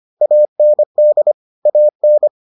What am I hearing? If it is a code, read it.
Morse code, heard as a keyed tone, about 25 wpm, 600 Hz. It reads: AND AN